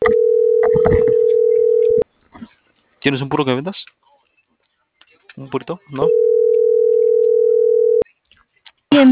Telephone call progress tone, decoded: ringback tone

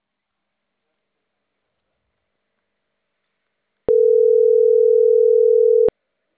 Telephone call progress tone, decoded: ringback tone

